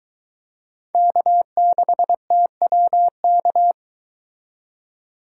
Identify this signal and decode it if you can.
Morse code, heard as a keyed tone, about 22 wpm, 700 Hz. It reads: K6TWK